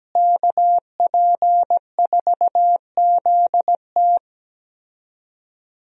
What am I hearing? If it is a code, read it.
Morse code, heard as a keyed tone, about 17 wpm, 700 Hz. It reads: KP4ZT
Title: USA Call Signs - 5 Characters - Repeated 2x - 17wpm